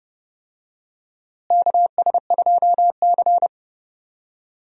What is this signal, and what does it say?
Morse code, heard as a keyed tone, about 32 wpm, 700 Hz. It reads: KS2C
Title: USA Call Signs - 4 Characters - Repeated 2x - 30wpm